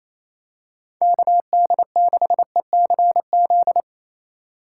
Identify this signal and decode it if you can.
Morse code, heard as a keyed tone, about 28 wpm, 700 Hz. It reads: KD6ECZ
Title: USA Call Signs - 6 Characters - 4x Character Spacing - 28wpm